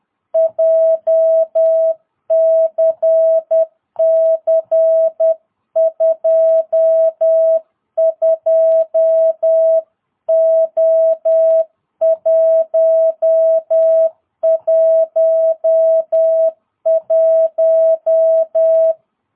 Morse code, heard as a keyed tone, about 10 wpm, 660 Hz. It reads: JCC 22O111
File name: jh8cbh / 2022cw